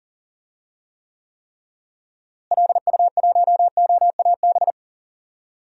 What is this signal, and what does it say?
Morse code, heard as a keyed tone, about 40 wpm, 700 Hz. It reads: LU1OAB
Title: DXCC-100 Call Signs - 6 Characters - Repeated 3x - 40wpm